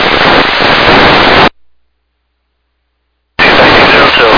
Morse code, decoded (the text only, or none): none